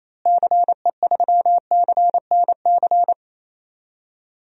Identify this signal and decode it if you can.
Morse code, heard as a keyed tone, about 28 wpm, 700 Hz. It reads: CE3CNC
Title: DXCC-100 Call Signs - 6 Characters - Repeated 3x - 28wpm